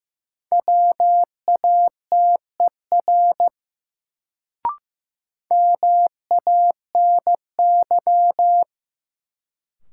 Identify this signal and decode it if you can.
Morse code, heard as a keyed tone, about 15 wpm, 700 Hz. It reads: WATER MANY
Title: Sentences from Top 100 Words - New Word Drill - 15wpm